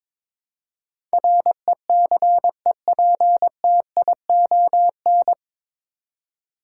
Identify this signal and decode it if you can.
Morse code, heard as a keyed tone, about 22 wpm, 700 Hz. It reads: RECEPTION
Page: Morse Code Ninja practice 22wpm